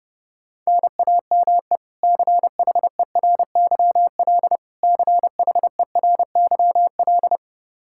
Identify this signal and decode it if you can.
Morse code, heard as a keyed tone, about 32 wpm, 700 Hz. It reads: NAME CHERYL CHERYL
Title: K1USN SST QSO Practice 30wpm